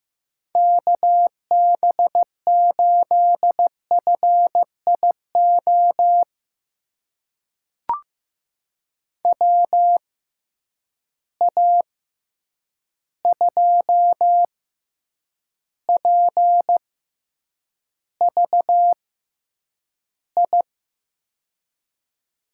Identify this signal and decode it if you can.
Morse code, heard as a keyed tone, about 15 wpm, spaced as for 8 wpm, 700 Hz. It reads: KB8FIO WA2PVI
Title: USA Call Signs - 6 Characters - 6x Character Spacing - 15wpm